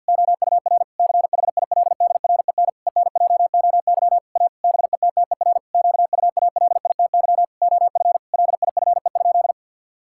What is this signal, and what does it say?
Morse code, heard as a keyed tone, about 50 wpm, 700 Hz. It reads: OUR CHILDREN ENJOY A BETTER QUALITY OF LIFE?
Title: Sentences from Top 1000 Words - Review Part 5 - 50wpm